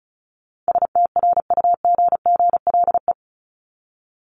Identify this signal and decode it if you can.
Morse code, heard as a keyed tone, about 34 wpm, 700 Hz. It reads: STRUGGLE
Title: Top 1751-2000 Words 35wpm